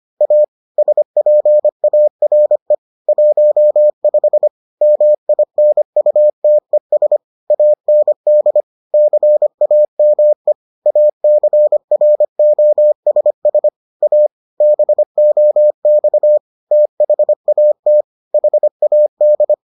Morse code, heard as a keyed tone, about 25 wpm, 600 Hz. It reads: A SPARE 15 MINUTES AND CAME ACROSS A BOX THAT HAD